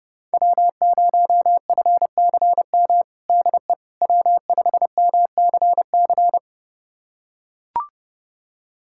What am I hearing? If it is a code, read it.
Morse code, heard as a keyed tone, about 30 wpm, 700 Hz. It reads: W0FCM DE W5MCC
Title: USA Callsign DE Callsign - 5 Characters - 30wpm